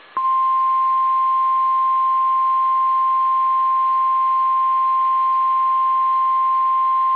Emergency Alert System (SAME) tones (the 1050 Hz attention tone). It is heard from 0.5 to 6.5 s.